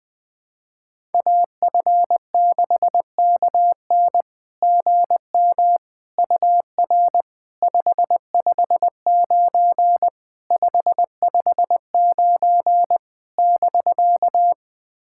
Morse code, heard as a keyed tone, about 20 wpm, 690 Hz. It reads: AF6KN GM UR 559 559 <BK>
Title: SOTA QSO Practice 20wpm